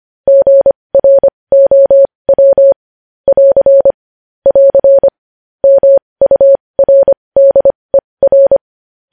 Morse code, heard as a keyed tone, about 25 wpm, spaced as for 20 wpm, 550 Hz. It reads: GROW <AR> <AR> MURDER
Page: news-headlines-in-morse-code-at-20-wpm